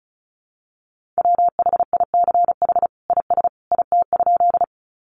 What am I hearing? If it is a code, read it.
Morse code, heard as a keyed tone, about 34 wpm, 700 Hz. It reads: WHICH IS IT?
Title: Sentences from Top 100 Words 35wpm